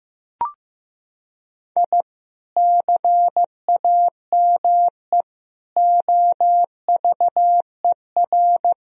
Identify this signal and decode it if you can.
Morse code, heard as a keyed tone, about 15 wpm, 700 Hz. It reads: I CAME OVER
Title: Sentences from Top 1000 Words - Review Part 3 - 15wpm